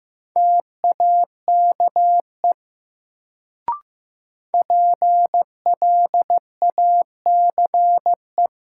Morse code, heard as a keyed tone, about 15 wpm, 700 Hz. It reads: TAKE PLACE
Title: Sentences from Top 200 Words - New Word Drill - 15wpm